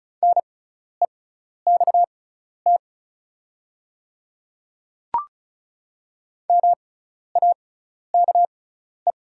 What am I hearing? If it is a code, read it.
Morse code, heard as a keyed tone, about 34 wpm, spaced as for 12 wpm, 690 Hz. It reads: NEXT MAKE
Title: 6x Character Spacing - Top 300 Words - 35wpm